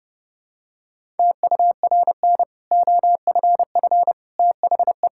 Morse code, heard as a keyed tone, about 30 wpm, 700 Hz. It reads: TURN OFF THE
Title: Sentences from Top 1000 Words - Review Part 6 - 30wpm